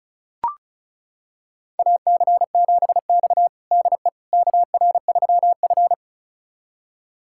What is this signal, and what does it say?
Morse code, heard as a keyed tone, about 36 wpm, 700 Hz. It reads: AC7X DE KR3F
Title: USA Callsign DE Callsign - 4 Characters - 35wpm